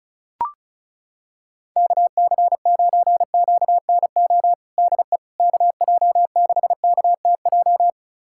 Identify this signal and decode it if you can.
Morse code, heard as a keyed tone, about 36 wpm, 700 Hz. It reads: KC9QNO DE KJ6KTJ
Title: USA Callsign DE Callsign 35wpm